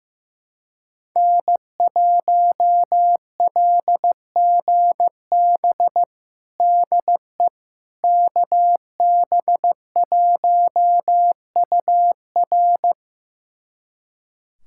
Morse code, heard as a keyed tone, about 15 wpm, 700 Hz. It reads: N1LGB DE KB1UR
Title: USA Callsign DE Callsign - 5 Characters - 15wpm